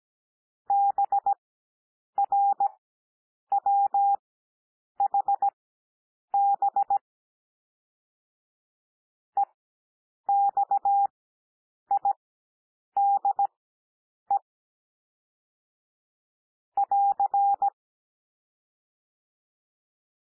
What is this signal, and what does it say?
Morse code, heard as a keyed tone, about 16 wpm, spaced as for 8 wpm, 800 Hz. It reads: BRWHB EXIDE <AR>
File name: cwmp3 / L2 SOUND LETTERS 17 wpm